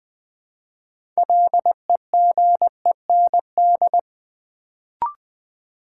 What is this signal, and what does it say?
Morse code, heard as a keyed tone, about 20 wpm, 700 Hz. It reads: LEGEND
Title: Build A Word - 3 Letters At A Time - 6-Letter Words - 20wpm